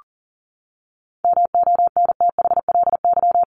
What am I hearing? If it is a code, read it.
Morse code, heard as a keyed tone, about 40 wpm, 700 Hz. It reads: MONTHLY
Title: LY Suffix - Top 100 Words with Base Words 40wpm